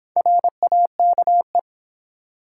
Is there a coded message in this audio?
Yes: Morse code from 0.2 to 1.6 s, heard as a keyed tone, about 26 wpm, 700 Hz. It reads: RAKE